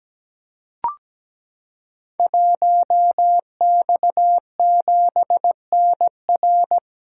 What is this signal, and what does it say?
Morse code, heard as a keyed tone, about 17 wpm, 700 Hz. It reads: 1X7NR
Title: Code Groups - 5 Letter:Numbers - 17wpm